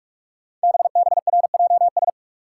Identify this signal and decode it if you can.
Morse code, heard as a keyed tone, about 44 wpm, 700 Hz. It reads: BBRJS
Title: Code Groups - 5 Letters - 45wpm